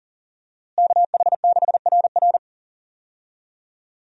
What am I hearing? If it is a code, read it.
Morse code, heard as a keyed tone, about 37 wpm, 700 Hz. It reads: KH6RR
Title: USA Call Signs - 5 Characters - Repeated 2x - 40wpm